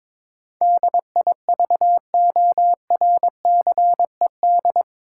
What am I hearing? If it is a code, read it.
Morse code, heard as a keyed tone, about 22 wpm, 700 Hz. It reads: DIVORCED
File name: practice / 22wpm / Eight Letter Words - 22wpm